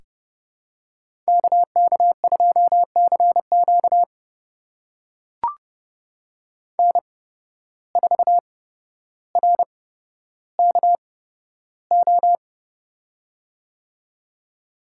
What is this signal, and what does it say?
Morse code, heard as a keyed tone, about 29 wpm, spaced as for 13 wpm, 700 Hz. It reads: KK2CQ N4RKO
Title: USA Call Signs - 5 Characters - 8x Character Spacing - 30wpm